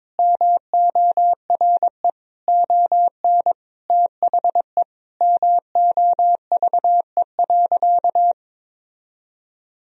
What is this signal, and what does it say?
Morse code, heard as a keyed tone, about 21 wpm, 700 Hz. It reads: MORE ON THE MOVE.
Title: Sentences from Top 1000 Words - Review Part 3 - 22wpm